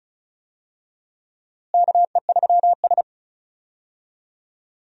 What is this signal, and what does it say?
Morse code, heard as a keyed tone, about 36 wpm, 700 Hz. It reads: KE3S
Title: USA Call Signs - 4 Characters - Repeated 2x - 35wpm